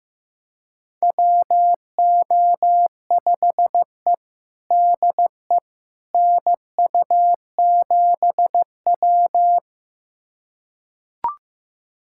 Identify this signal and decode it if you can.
Morse code, heard as a keyed tone, about 15 wpm, 700 Hz. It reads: WO5E DE NU7W
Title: USA Callsign DE Callsign - 4 Characters - 15wpm